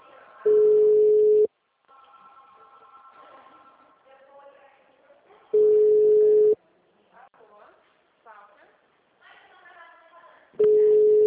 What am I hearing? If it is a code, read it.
Ringback tone (Europe, 425 Hz), with speech.
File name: out- -2004- - - .4424